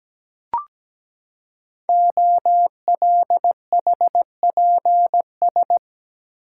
Morse code, heard as a keyed tone, about 17 wpm, 700 Hz. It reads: OLHPS
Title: Code Groups - 5 Letters - 17wpm